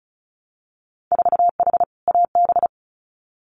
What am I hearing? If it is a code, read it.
Morse code, heard as a keyed tone, about 36 wpm, 700 Hz. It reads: 4H AB
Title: Winter Field Day Exchanges Encore 35wpm